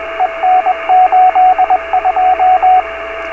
Morse code, heard as a keyed tone, about 21 wpm, 740 Hz, with radio noise. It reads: EN82
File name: beacons / k8uk